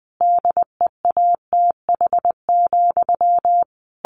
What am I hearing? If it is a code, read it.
Morse code, heard as a keyed tone, about 20 wpm, 700 Hz. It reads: DEATH,
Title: A Sound Of Thunder 20wpm